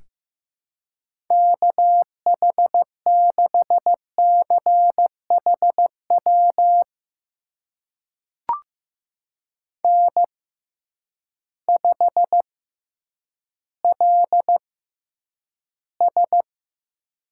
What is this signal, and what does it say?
Morse code, heard as a keyed tone, about 15 wpm, spaced as for 9 wpm, 700 Hz. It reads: KH6CHW N5LS
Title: USA Call Signs - 6x Character Spacing - 15wpm